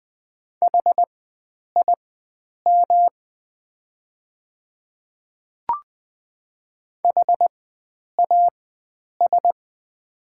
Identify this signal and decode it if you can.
Morse code, heard as a keyed tone, about 20 wpm, spaced as for 9 wpm, 700 Hz. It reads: HIM HAS
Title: 4x Character Spacing - Top 300 Words - 20wpm